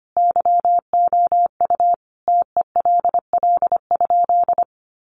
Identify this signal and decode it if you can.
Morse code, heard as a keyed tone, about 25 wpm, 700 Hz. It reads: YOU TELL?
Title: Sentences from Top 300 Words 25wpm